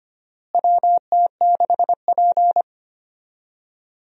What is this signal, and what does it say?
Morse code, heard as a keyed tone, about 25 wpm, 700 Hz. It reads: WT6P
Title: USA Call Signs - 4 Characters - Repeated 2x - 25wpm